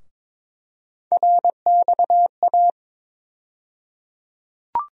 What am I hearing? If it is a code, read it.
Morse code, heard as a keyed tone, about 22 wpm, 700 Hz. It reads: RXA
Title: USA Call Sign Postfixes - 3 Letters 22wpm